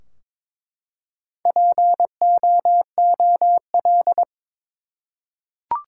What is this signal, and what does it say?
Morse code, heard as a keyed tone, about 22 wpm, 700 Hz. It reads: POOL
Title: Four Letter Words - 22wpm